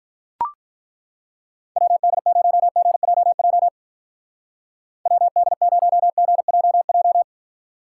Morse code, heard as a keyed tone, about 41 wpm, 700 Hz. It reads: WD0GJJ WD0GJJ
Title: Speed Racing - USA 2x3 Call Signs 35wpm